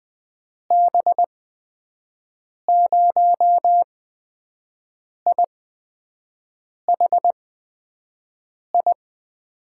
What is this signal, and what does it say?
Morse code, heard as a keyed tone, about 20 wpm, spaced as for 5 wpm, 700 Hz. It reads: B0IHI
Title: USA Call Signs - 6 Characters - 8x Character Spacing - 20wpm